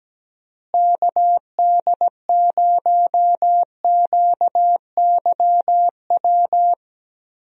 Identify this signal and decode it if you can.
Morse code, heard as a keyed tone, about 17 wpm, 700 Hz. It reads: KD0QYW
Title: USA Call Signs - 6 Characters - Repeated 2x - 17wpm